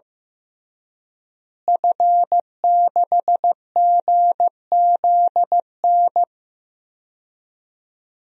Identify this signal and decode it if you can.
Morse code, heard as a keyed tone, about 15 wpm, 700 Hz. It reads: F6GZN
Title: DXCC-100 Call Signs - 5 Characters - Repeated 3x - 15wpm